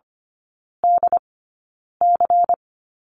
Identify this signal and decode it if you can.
Morse code, heard as a keyed tone, about 25 wpm, spaced as for 9 wpm, 700 Hz. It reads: DC
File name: course / 25:10wpm / 093 Lesson 16e - Letter c - Three Characters 25wpm